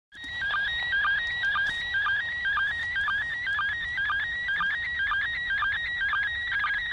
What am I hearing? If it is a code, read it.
SSTV (slow-scan television), PD 120.
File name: spazio / ISS 145 800MHz SSTV 4